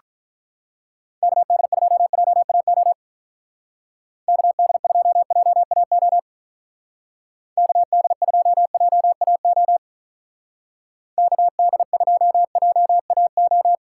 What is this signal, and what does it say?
Morse code, heard as a keyed tone, about 43 wpm, 700 Hz. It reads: KD2JAO KD2JAO KD2JAO KD2JAO
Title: Speed Racing - USA 2x3 Call Signs 35wpm